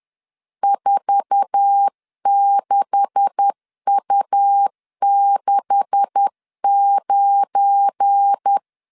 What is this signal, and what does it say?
Morse code, heard as a keyed tone, about 10 wpm, 800 Hz. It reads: 46U69
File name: LEZIONI / 9 286 10wpm 2735